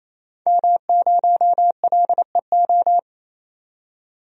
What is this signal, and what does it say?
Morse code, heard as a keyed tone, about 28 wpm, 700 Hz. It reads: M0LEO
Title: DXCC-100 Call Signs - 5 Characters - 28wpm